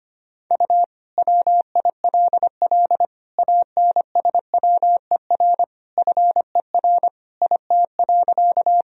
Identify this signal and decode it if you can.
Morse code, heard as a keyed tone, about 25 wpm, 700 Hz. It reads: U WILL ANSWER FER IT.
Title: Sentences from Top 300 Words and Abbreviations 25wpm